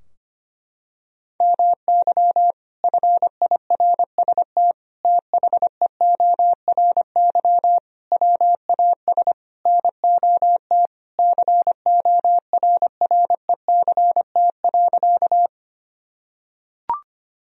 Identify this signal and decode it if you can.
Morse code, heard as a keyed tone, about 25 wpm, 700 Hz. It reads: MY FIRST THEORY WAS NOT CORRECT.
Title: Sentences from Top 1000 Words - Review Part 3 - 25wpm